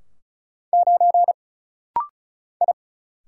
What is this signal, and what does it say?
Morse code, heard as a keyed tone, about 36 wpm, 700 Hz. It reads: 9 I
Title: Single Letter-Number - Rapid-Fire 35wpm